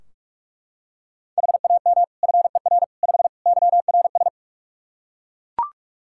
Morse code, heard as a keyed tone, about 41 wpm, 700 Hz. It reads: HAM FER 5 YRS
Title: K1USN SST QSO Practice 45wpm